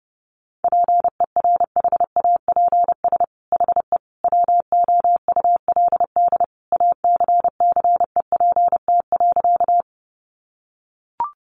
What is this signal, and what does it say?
Morse code, heard as a keyed tone, about 30 wpm, 700 Hz. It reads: PERHAPS HE WOULD ACCEPT.
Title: Sentences from Top 1300 Words 30wpm